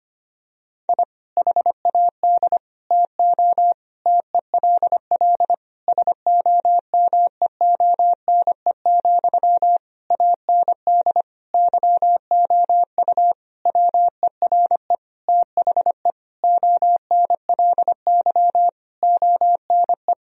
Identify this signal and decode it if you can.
Morse code, heard as a keyed tone, about 25 wpm, 700 Hz. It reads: I HAD TO TELL SOMEONE, AND YOU WERE THE ONLY ONE
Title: Sentences from Top 1000 Words - Review Part 4 - 25wpm